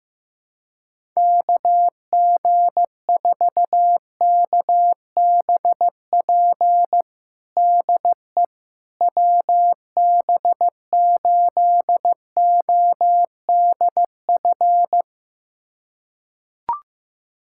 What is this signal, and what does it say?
Morse code, heard as a keyed tone, about 16 wpm, 700 Hz. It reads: KG4KBP DE WB8ODF
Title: USA Callsign DE Callsign 15wpm